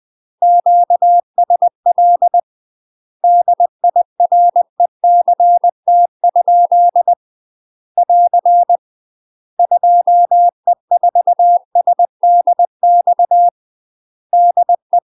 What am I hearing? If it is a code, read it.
Morse code, heard as a keyed tone, about 20 wpm, 700 Hz. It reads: QSL DIRECT? <AR> 2E4SDX DE
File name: MASTER QSOs TRAINING/20WPM